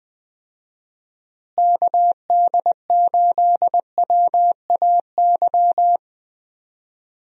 Morse code, heard as a keyed tone, about 20 wpm, 700 Hz. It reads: KD8WAY
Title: USA Call Signs - 6 Characters - Repeated 2x - 20wpm